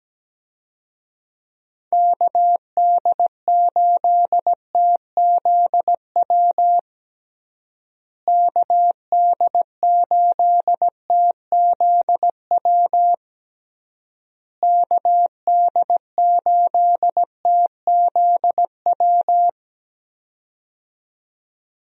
Morse code, heard as a keyed tone, about 17 wpm, 700 Hz. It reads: KD8TZW KD8TZW KD8TZW
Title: USA Call Signs - 6 Characters - Repeated 3x - 17wpm